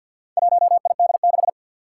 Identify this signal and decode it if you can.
Morse code, heard as a keyed tone, about 51 wpm, 700 Hz. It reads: 1ID6
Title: Four Challenging Letter-Number Combinations 50wpm